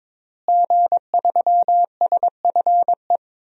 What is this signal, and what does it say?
Morse code, heard as a keyed tone, about 22 wpm, 700 Hz. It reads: G3SFE
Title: DXCC-100 Call Signs - 5 Characters - Repeated 2x - 22wpm